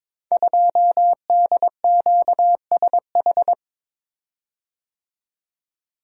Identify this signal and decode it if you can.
Morse code, heard as a keyed tone, about 22 wpm, 700 Hz. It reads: 2DQSH
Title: Code Groups - 5 Letter:Numbers - 22wpm